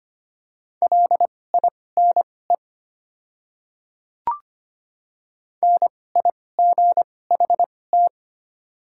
Morse code, heard as a keyed tone, about 25 wpm, spaced as for 18 wpm, 690 Hz. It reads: LINE NIGHT
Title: 2x Character Spacing - Top 300 Words - 25wpm